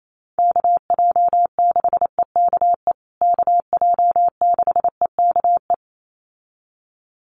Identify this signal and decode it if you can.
Morse code, heard as a keyed tone, about 28 wpm, 700 Hz. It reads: KJ6EKE KJ6EKE
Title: USA Call Signs Repeated 2x 28wpm